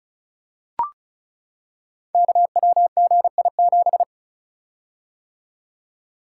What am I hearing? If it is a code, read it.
Morse code, heard as a keyed tone, about 36 wpm, 700 Hz. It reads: KWGI7
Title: Code Groups - 5 Letter:Numbers - 35wpm